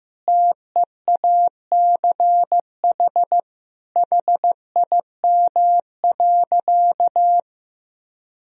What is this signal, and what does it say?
Morse code, heard as a keyed tone, about 15 wpm, 700 Hz. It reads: TEACH HIM.
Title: Sentences from Top 1000 Words - Review Part 2 - 15wpm